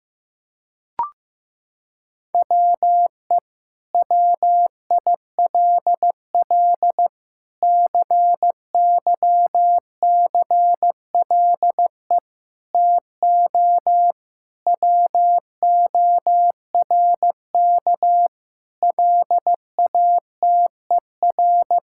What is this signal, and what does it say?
Morse code, heard as a keyed tone, about 15 wpm, 700 Hz. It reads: WE WILL CYCLE TO WORK LATER
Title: Sentences from Top 1000 Words - Review Part 6 - 15wpm